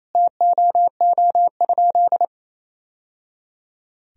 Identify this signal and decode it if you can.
Morse code, heard as a keyed tone, about 28 wpm, 700 Hz. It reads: TOO?
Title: Sentences from Top 1000 Words - Review Part 1 - 28wpm